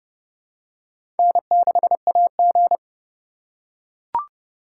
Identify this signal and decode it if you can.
Morse code, heard as a keyed tone, about 30 wpm, 700 Hz. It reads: N6AG
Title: USA Call Signs - 6x Character Spacing - 30wpm